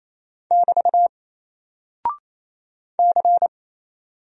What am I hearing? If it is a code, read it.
Morse code, heard as a keyed tone, about 28 wpm, 700 Hz. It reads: <BT> C
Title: Single Character (Letter:Number:Punctuation:Prosign) Alt 28wpm